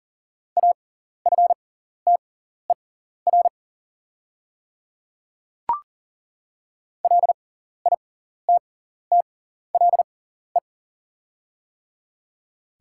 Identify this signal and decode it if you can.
Morse code, heard as a keyed tone, about 40 wpm, spaced as for 14 wpm, 700 Hz. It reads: AFTER LITTLE
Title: 6x Character Spacing - Top 200 Words - 40wpm